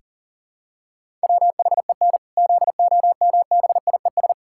Morse code, heard as a keyed tone, about 40 wpm, 700 Hz. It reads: WHEN ZOMBIES
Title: Pangram Challenge v2 - 40wpm